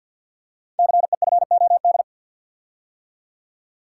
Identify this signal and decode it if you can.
Morse code, heard as a keyed tone, about 51 wpm, 700 Hz. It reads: CEFOD
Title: Code Groups - 5 Letters - 50wpm